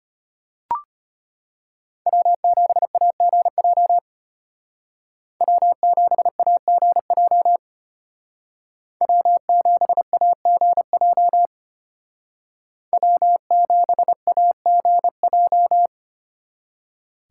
Morse code, heard as a keyed tone, about 33 wpm, 700 Hz. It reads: W7AGJ W7AGJ W7AGJ W7AGJ